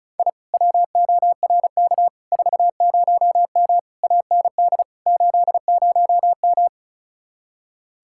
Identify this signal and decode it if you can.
Morse code, heard as a keyed tone, about 44 wpm, 700 Hz. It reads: I WORK 40M AND 80M
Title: Speed-Racing Common QSO Phrases Encore - 35wpm